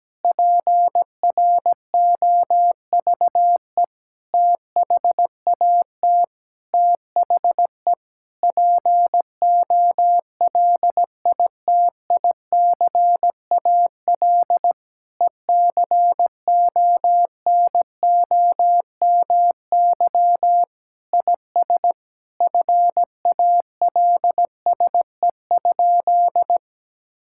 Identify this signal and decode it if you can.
Morse code, heard as a keyed tone, about 17 wpm, 700 Hz. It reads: PROVE THAT THE POLITICAL ECONOMY IS FALSE?
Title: Sentences from Top 1600 Words 17wpm